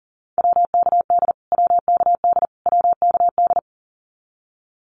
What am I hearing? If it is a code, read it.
Morse code, heard as a keyed tone, about 40 wpm, 700 Hz. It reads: WKD WKD WKD
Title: Abbreviations Repeated 3x - 40wpm